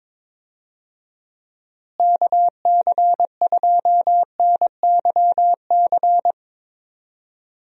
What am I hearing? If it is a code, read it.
Morse code, heard as a keyed tone, about 22 wpm, 700 Hz. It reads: KC2NYC
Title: USA Call Signs - 6 Characters - Repeated 3x - 22wpm